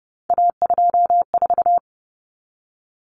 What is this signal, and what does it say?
Morse code, heard as a keyed tone, about 30 wpm, 700 Hz. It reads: A24
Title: Common Cut Numbers - 3-Digit 30wpm